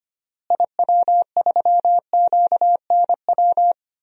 Morse code, heard as a keyed tone, about 25 wpm, 700 Hz. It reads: IW3QNW
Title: DXCC-100 Call Signs - 6 Characters - Repeated 3x - 25wpm